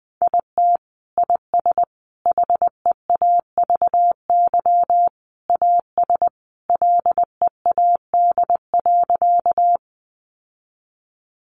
Morse code, heard as a keyed tone, about 20 wpm, 700 Hz. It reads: IT IS HEAVY AS LEAD.
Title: Sentences from Top 1100 Words 20wpm